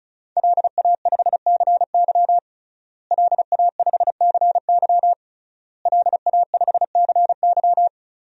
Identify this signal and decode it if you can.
Morse code, heard as a keyed tone, about 36 wpm, 700 Hz. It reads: LA5CY LA5CY LA5CY
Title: DXCC-100 Call Signs - 5 Characters - Repeated 3x - 35wpm